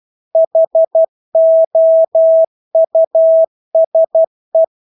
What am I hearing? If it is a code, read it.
Morse code, heard as a keyed tone, about 12 wpm, 650 Hz. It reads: HOUSE
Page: Engelske ord 12 wpm